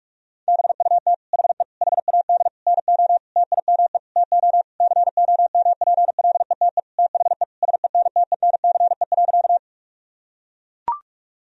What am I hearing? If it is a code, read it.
Morse code, heard as a keyed tone, about 43 wpm, 700 Hz. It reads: BUT HE HAD NO TIME TO COMPLETE THE SENTENCE.
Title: Sentences from Top 1000 Words - Review Part 1 - 45wpm